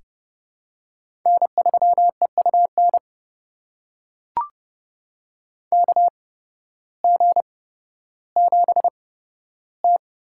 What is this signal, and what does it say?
Morse code, heard as a keyed tone, about 30 wpm, spaced as for 14 wpm, 700 Hz. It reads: N3EUN KG7T
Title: USA Call Signs - 8x Character Spacing - 30wpm